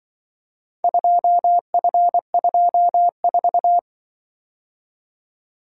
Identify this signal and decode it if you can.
Morse code, heard as a keyed tone, about 24 wpm, 700 Hz. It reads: 2F24
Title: Speed-Racing - Four Letter-Number Combinations - 20wpm